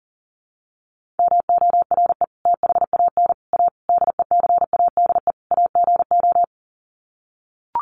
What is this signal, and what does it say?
Morse code, heard as a keyed tone, about 40 wpm, 700 Hz. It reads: MORE THAN A DECADE AGO
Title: Sets of 5 Words Encore2 40wpm